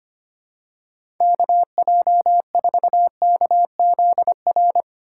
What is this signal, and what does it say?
Morse code, heard as a keyed tone, about 25 wpm, 700 Hz. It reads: KJ4KZR
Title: USA Call Signs - 6 Characters - 2x Character Spacing - 25wpm